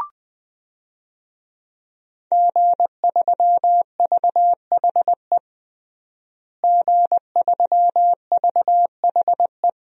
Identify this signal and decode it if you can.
Morse code, heard as a keyed tone, about 20 wpm, 700 Hz. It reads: G3VHE G3VHE
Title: DXCC-100 Call Signs - 5 Characters - Repeated 2x - 20wpm